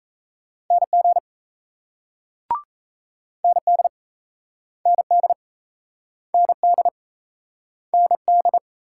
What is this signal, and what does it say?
Morse code, heard as a keyed tone, about 36 wpm, 700 Hz. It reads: NG ND ND ND ND